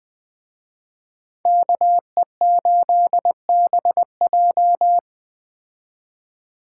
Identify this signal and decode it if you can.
Morse code, heard as a keyed tone, about 20 wpm, 700 Hz. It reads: KE8BJ
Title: Speed Racing - USA 2x2 Call Signs 17wpm